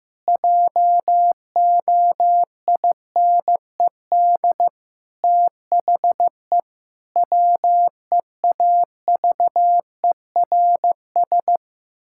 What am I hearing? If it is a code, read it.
Morse code, heard as a keyed tone, about 15 wpm, 700 Hz. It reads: JOINED THE WEAVERS
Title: Pangram Challenge v2 - 15wpm